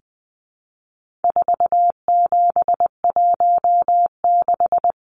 Morse code, heard as a keyed tone, about 20 wpm, 700 Hz. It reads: 4716
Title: Numbers - 4-Digit 20wpm